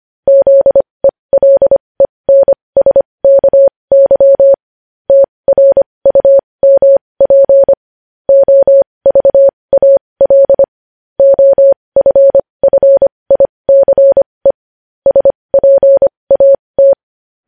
Morse code, heard as a keyed tone, about 25 wpm, spaced as for 20 wpm, 550 Hz. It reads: ZELENSKY TRUMP OVAL OFFICE SPAT
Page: news-headlines-in-morse-code-at-20-wpm